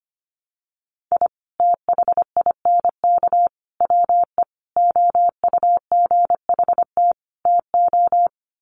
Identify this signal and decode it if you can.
Morse code, heard as a keyed tone, about 25 wpm, 700 Hz. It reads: I THINK WE OUGHT TO
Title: Sets of 5 Words Encore 25wpm